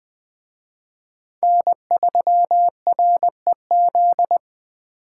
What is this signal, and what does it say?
Morse code, heard as a keyed tone, about 20 wpm, 700 Hz. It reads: N3REZ